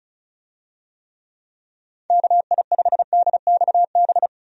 Morse code, heard as a keyed tone, about 34 wpm, 700 Hz. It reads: KI5DXB
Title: USA Call Signs - 6 Characters - 35wpm